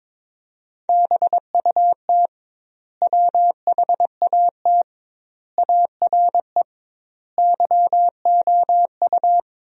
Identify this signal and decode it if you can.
Morse code, heard as a keyed tone, about 21 wpm, 700 Hz. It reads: BUT WHAT ARE YOU
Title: 2x Word-Spacing - Sentences from Top 100 Words - 22wpm